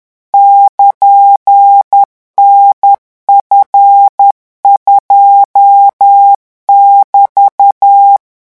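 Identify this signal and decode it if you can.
Morse code, heard as a keyed tone, about 10 wpm, 800 Hz. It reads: <KN>NF2<BT>
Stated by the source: French Morse code revision lesson 18 at 10 words per minute.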